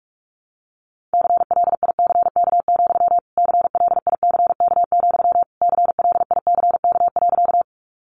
Morse code, heard as a keyed tone, about 44 wpm, 700 Hz. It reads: CLICK, CLICK, CLICK.
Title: Sentences from Top 1100 Words 45wpm